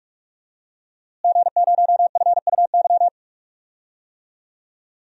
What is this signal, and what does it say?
Morse code, heard as a keyed tone, about 47 wpm, 700 Hz. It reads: G0FUY
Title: DXCC-100 Call Signs - 5 Characters - Repeated 2x - 45wpm